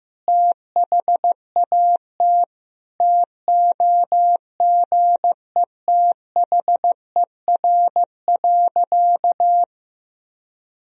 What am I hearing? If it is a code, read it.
Morse code, heard as a keyed tone, about 15 wpm, 700 Hz. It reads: THAT TOGETHER.
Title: Sentences from Top 1000 Words - Review Part 1 - 15wpm